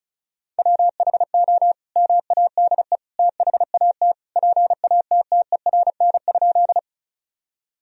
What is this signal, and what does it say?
Morse code, heard as a keyed tone, about 34 wpm, 700 Hz. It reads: WHO MADE THAT PATTERN?
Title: Sentences from Top 1000 Words - Review Part 2 - 35wpm